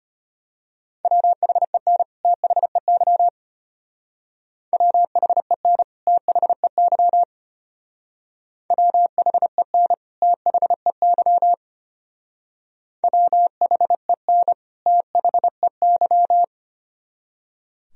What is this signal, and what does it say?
Morse code, heard as a keyed tone, about 32 wpm, 700 Hz. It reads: WHEN THEY WHEN THEY WHEN THEY WHEN THEY